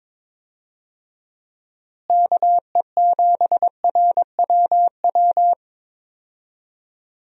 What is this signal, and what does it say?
Morse code, heard as a keyed tone, about 23 wpm, 700 Hz. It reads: KE7RWW
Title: USA Call Signs - 6 Characters - Repeated 2x - 22wpm